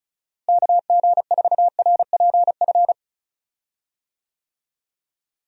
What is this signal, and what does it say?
Morse code, heard as a keyed tone, about 36 wpm, 700 Hz. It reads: KG4RPF
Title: USA Call Signs - 6 Characters - Repeated 3x - 35wpm